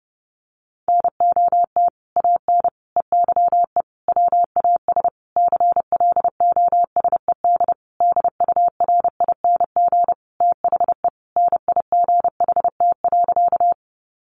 Morse code, heard as a keyed tone, about 28 wpm, 700 Hz. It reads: NOT AN EYE WAS CLOSED DURING THE NIGHT.
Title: Sentences from Top 1400 Words 30wpm